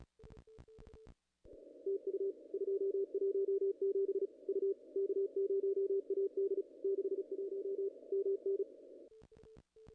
Morse code, heard as a keyed tone, about 35 wpm, 400 Hz, with radio noise. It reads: TU 217 U K0AD 61 MN
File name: contests / ss-cw-k5tr-op-n5rz -2300